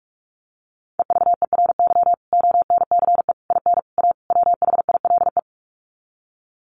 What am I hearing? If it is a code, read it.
Morse code, heard as a keyed tone, about 44 wpm, 700 Hz. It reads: EVERY ONCE IN A WHILE
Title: Sets of 5 Words 45wpm